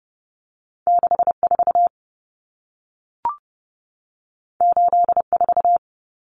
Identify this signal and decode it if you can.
Morse code, heard as a keyed tone, about 30 wpm, 700 Hz. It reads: 64 84
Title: Numbers - 2-Digit 30wpm